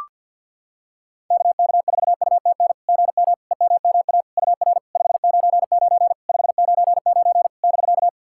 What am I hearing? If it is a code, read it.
Morse code, heard as a keyed tone, about 49 wpm, 690 Hz. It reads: KK4UTN GM EMMA UR 599 599 <BK>
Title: SOTA QSO Practice 50wpm